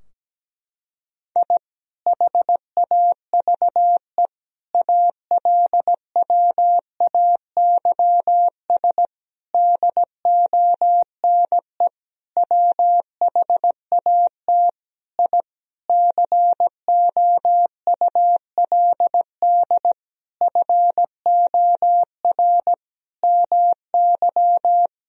Morse code, heard as a keyed tone, about 17 wpm, 700 Hz. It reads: I HAVE ALWAYS DONE WHAT I COULD FOR MY
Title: Sentences from Top 1000 Words - Review Part 2 - 17wpm